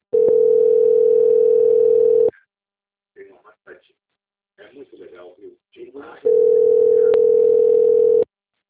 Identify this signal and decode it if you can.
Ringback tone (North America), 440 + 480 Hz, with speech.
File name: out- - - - - .45676